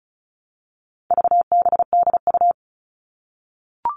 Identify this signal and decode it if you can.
Morse code, heard as a keyed tone, about 34 wpm, 700 Hz. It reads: VBDU
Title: All Cut Numbers - 4-Digit 35wpm